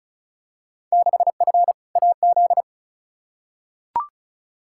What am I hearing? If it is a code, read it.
Morse code, heard as a keyed tone, about 36 wpm, 700 Hz. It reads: 6F AZ
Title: Field Day Exchanges Encore - 35wpm